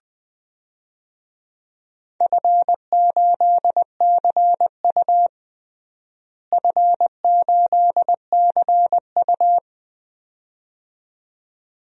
Morse code, heard as a keyed tone, about 21 wpm, 700 Hz. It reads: F8CU F8CU
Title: DXCC-100 Call Signs - 4 Characters - Repeated 2x - 20wpm